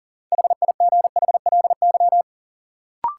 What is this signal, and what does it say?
Morse code, heard as a keyed tone, about 40 wpm, 700 Hz. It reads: HIGHLY
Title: Sentences from Top 1100 Words - New Word Drill Alt - 40wpm